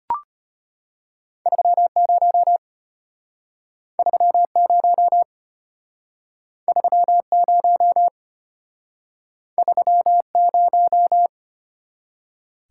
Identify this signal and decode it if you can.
Morse code, heard as a keyed tone, about 32 wpm, 690 Hz. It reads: 30 30 30 30